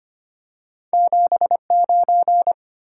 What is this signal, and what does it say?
Morse code, heard as a keyed tone, about 25 wpm, 700 Hz. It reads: 79
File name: Two Letter-Number Combinations - 25wpm